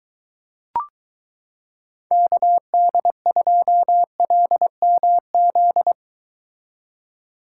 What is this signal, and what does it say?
Morse code, heard as a keyed tone, about 23 wpm, 700 Hz. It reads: KD2LMZ